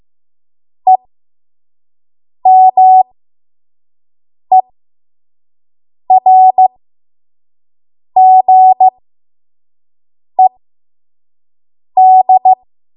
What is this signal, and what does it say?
Morse code, heard as a keyed tone, about 15 wpm, spaced as for 5 wpm, 750 Hz. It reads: EMERGED
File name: Morse / 05WPM